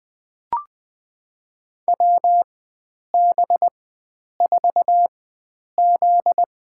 Morse code, heard as a keyed tone, about 20 wpm, spaced as for 9 wpm, 700 Hz. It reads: WB4Z
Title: USA Call Signs - 4x Character Spacing - 20wpm